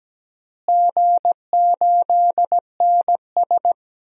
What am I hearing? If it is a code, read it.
Morse code, heard as a keyed tone, about 17 wpm, 700 Hz. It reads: G8NS
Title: DXCC-100 Call Signs - 4 Characters - Repeated 2x - 17wpm